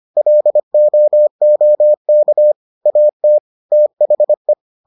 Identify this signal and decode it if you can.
Morse code, heard as a keyed tone, about 25 wpm, 600 Hz. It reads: LOOK AT THE